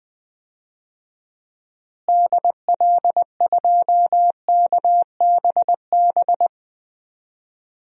Morse code, heard as a keyed tone, about 20 wpm, 700 Hz. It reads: DL2KBB
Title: DXCC-100 Call Signs - 6 Characters - Repeated 3x - 20wpm